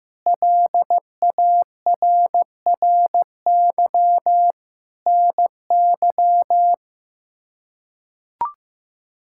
Morse code, heard as a keyed tone, about 15 wpm, 700 Hz. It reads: LARRY NY
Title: K1USN SST QSO Practice - 15wpm